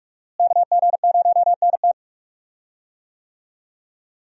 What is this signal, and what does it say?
Morse code, heard as a keyed tone, about 44 wpm, 700 Hz. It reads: KG0NT
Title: USA Call Signs - 5 Characters - Repeated 2x - 45wpm